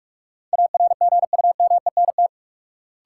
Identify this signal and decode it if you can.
Morse code, heard as a keyed tone, about 41 wpm, 700 Hz. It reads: ARGUMENT
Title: Sentences from Top 1400 Words - New Word Drill Alt - 45wpm